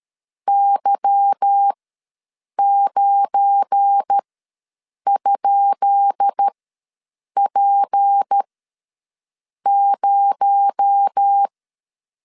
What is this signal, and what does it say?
Morse code, heard as a keyed tone, about 13 wpm, spaced as for 7 wpm, 800 Hz. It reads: Y9?P0
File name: LEZIONI / 10 SIGNS / 10D COMA ws 12wpm 3618